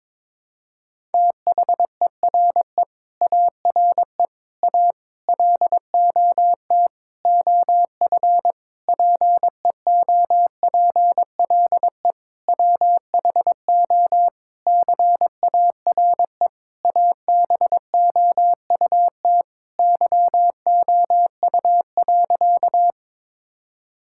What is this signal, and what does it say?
Morse code, heard as a keyed tone, about 22 wpm, 700 Hz. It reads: THERE ARE A LOT OF PEOPLE WHO CARE ABOUT YOU.
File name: Sentences with 7-Word Phrases - 22wpm